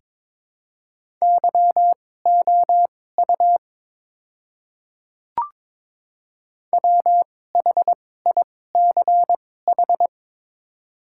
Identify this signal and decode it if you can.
Morse code, heard as a keyed tone, about 22 wpm, spaced as for 16 wpm, 700 Hz. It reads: YOU WHICH
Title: 2x Character Spacing - Top 200 Words - 22wpm